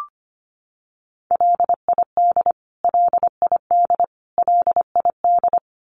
Morse code, heard as a keyed tone, about 25 wpm, 700 Hz. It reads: LID LID LID
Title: Abbreviations Repeated 3x - 25wpm